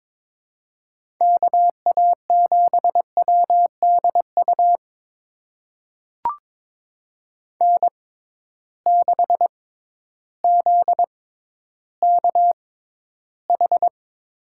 Morse code, heard as a keyed tone, about 22 wpm, spaced as for 12 wpm, 700 Hz. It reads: KA7WDU N6ZKH
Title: USA Call Signs - 6x Character Spacing - 22wpm